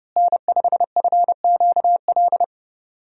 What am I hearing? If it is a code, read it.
Morse code, heard as a keyed tone, about 31 wpm, 700 Hz. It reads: N5FQL
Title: USA Call Signs - 5 Characters - Repeated 2x - 30wpm